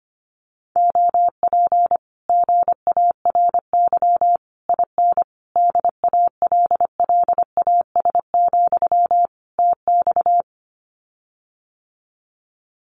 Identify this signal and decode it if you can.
Morse code, heard as a keyed tone, about 25 wpm, 700 Hz. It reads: OP GARY IN DALLAS, TX
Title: Common QSO Phrases Encore 25wpm